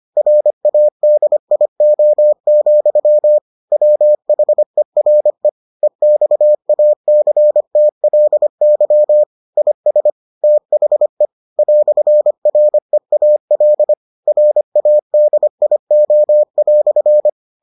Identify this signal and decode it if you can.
Morse code, heard as a keyed tone, about 25 wpm, 600 Hz. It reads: RADIO, WHERE EXACTLY IS THE "REAL RADIO"